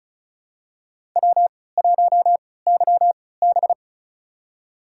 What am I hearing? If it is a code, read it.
Morse code, heard as a keyed tone, about 34 wpm, spaced as for 20 wpm, 700 Hz. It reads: W1YB
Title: USA Call Signs - 4 Characters - 3x Character Spacing - 35wpm